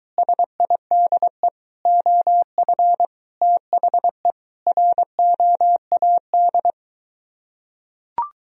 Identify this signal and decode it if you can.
Morse code, heard as a keyed tone, about 23 wpm, 700 Hz. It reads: SIDE OF THE ROAD